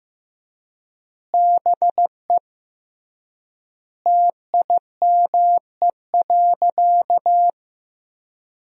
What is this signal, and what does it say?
Morse code, heard as a keyed tone, about 15 wpm, 700 Hz. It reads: BE TIME.
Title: 3x Word-Spacing - Sentences from Top 200 Words - 15wpm